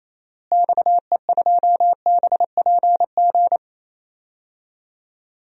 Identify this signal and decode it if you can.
Morse code, heard as a keyed tone, about 28 wpm, 700 Hz. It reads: XE2BPG
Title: DXCC-100 Call Signs - 6 Characters - Repeated 3x - 28wpm